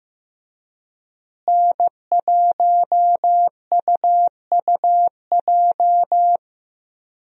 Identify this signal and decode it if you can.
Morse code, heard as a keyed tone, about 15 wpm, 700 Hz. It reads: N1UUJ
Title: USA Call Signs - 5 Characters - Repeated 3x - 15wpm